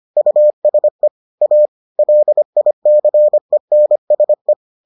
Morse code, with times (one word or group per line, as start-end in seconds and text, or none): 0.2-1.1 USE
1.4-1.7 A
2.0-4.5 LICENSE